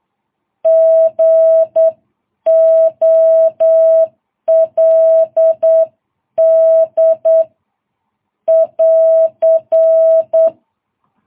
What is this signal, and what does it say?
Morse code, heard as a keyed tone, about 9 wpm, 650 Hz. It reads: GOLD <AR>
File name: jh8cbh